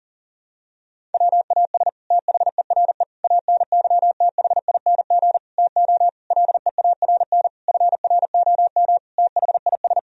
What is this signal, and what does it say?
Morse code, heard as a keyed tone, about 40 wpm, 700 Hz. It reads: WAS THERE ANYTHING TO LEARN FROM THIS
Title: Sentences from Top 1000 Words - Review Part 1 - 40wpm